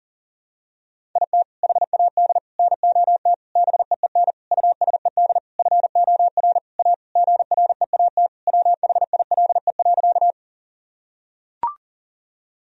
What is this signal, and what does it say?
Morse code, heard as a keyed tone, about 40 wpm, 700 Hz. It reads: IT HAD NOT BEEN USED FOR A GREAT WHILE.
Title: Sentences from Top 1000 Words - Review Part 1 - 40wpm